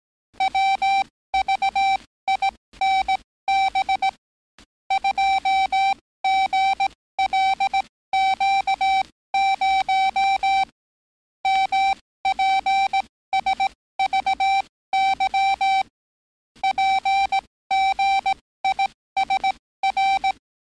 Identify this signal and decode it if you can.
Morse code, heard as a keyed tone, about 18 wpm, spaced as for 15 wpm, 780 Hz. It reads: WVINB 2GLQ0 MPSVY PGISR
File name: training / NAVPERS 11049to11060 / NAVPERS 11056 RA RB